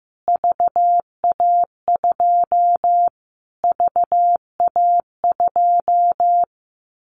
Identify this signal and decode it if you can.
Morse code, heard as a keyed tone, about 15 wpm, 700 Hz. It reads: VA2 VA2
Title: Callsign Prefixes for Top 25 DXCC - Repeated 3x - 15wpm